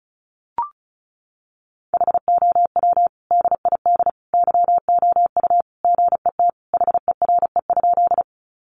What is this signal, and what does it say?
Morse code, heard as a keyed tone, about 34 wpm, 700 Hz. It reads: HOW DID YOU GET HERE?
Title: Sentences from Top 200 Words 35wpm